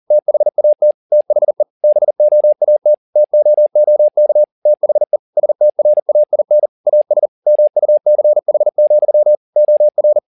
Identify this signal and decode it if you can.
Morse code, heard as a keyed tone, about 40 wpm, 600 Hz. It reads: THAT THE BOAT TOOK THE STRAIN AS MUCH, OR
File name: the old man and the sea 2 40wpm